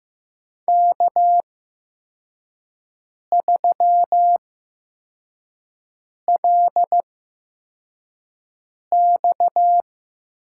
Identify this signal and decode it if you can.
Morse code, heard as a keyed tone, about 15 wpm, spaced as for 4 wpm, 700 Hz. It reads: K3LX
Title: USA Call Signs - 4 Characters - 8x Character Spacing - 15wpm